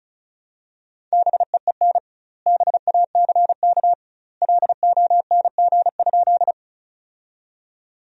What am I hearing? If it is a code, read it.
Morse code, heard as a keyed tone, about 34 wpm, 700 Hz. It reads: BEEN BACK LONG?
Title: 2x Word-Spacing - Sentences from Top 200 Words - 35wpm